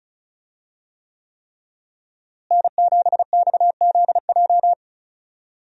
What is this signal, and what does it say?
Morse code, heard as a keyed tone, about 34 wpm, 700 Hz. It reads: N7XZJ
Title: USA Call Signs - 5 Characters - Repeated 3x - 35wpm